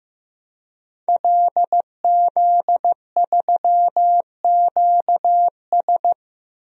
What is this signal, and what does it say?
Morse code, heard as a keyed tone, about 15 wpm, 700 Hz. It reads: LZ3QS
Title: DXCC-100 Call Signs - 5 Characters - Repeated 3x - 15wpm